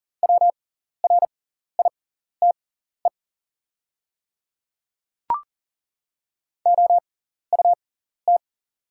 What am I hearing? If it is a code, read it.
Morse code, heard as a keyed tone, about 39 wpm, spaced as for 14 wpm, 700 Hz. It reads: WRITE OUT
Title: 6x Character Spacing - Top 200 Words - 40wpm